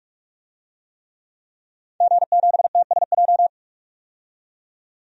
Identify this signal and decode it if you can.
Morse code, heard as a keyed tone, about 46 wpm, 700 Hz. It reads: G7TSJ
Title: DXCC-100 Call Signs - 5 Characters - 45wpm